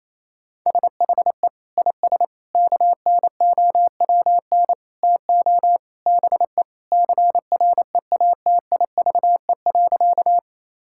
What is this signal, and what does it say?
Morse code, heard as a keyed tone, about 28 wpm, 700 Hz. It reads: SHE IS KNOWN TO BE CREATIVE.
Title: Sentences from Top 1000 Words - Review Part 5 - 28wpm